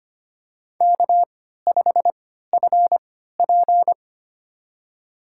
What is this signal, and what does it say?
Morse code, heard as a keyed tone, about 25 wpm, spaced as for 14 wpm, 700 Hz. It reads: K5FP
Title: USA Call Signs - 3x Character Spacing - 25wpm